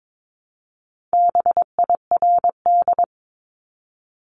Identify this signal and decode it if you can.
Morse code, heard as a keyed tone, about 22 wpm, 700 Hz. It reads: BIRD
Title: Top 500 Words 22wpm